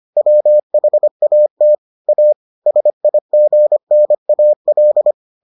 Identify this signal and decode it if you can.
Morse code, heard as a keyed tone, about 25 wpm, 600 Hz. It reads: WHAT A SIGNAL